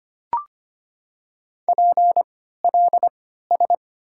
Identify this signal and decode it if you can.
Morse code, heard as a keyed tone, about 25 wpm, spaced as for 14 wpm, 690 Hz. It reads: PLS
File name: Top 90 QSO Elements - 3x Character Spacing - 25wpm